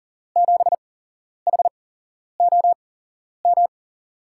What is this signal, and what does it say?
Morse code, heard as a keyed tone, about 39 wpm, spaced as for 11 wpm, 700 Hz. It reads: 7HOM
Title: USA Call Signs - 6 Characters - 8x Character Spacing - 40wpm